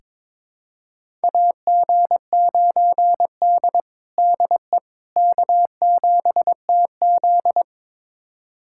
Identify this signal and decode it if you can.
Morse code, heard as a keyed tone, about 21 wpm, 700 Hz. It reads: AG9D DE K7TZ
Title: USA Callsign DE Callsign - 4 Characters - 22wpm